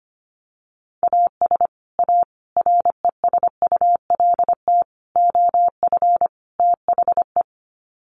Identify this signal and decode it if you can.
Morse code, heard as a keyed tone, about 25 wpm, 700 Hz. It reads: AS A RESULT OF THE
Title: Sets of 5 Words 25wpm